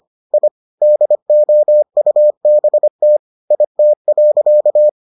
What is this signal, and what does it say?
Morse code, heard as a keyed tone, about 25 wpm, 600 Hz. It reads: I DOUBT IT.